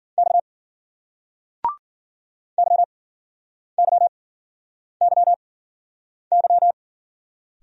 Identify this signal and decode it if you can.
Morse code, heard as a keyed tone, about 56 wpm, spaced as for 9 wpm, 700 Hz. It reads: X YYYY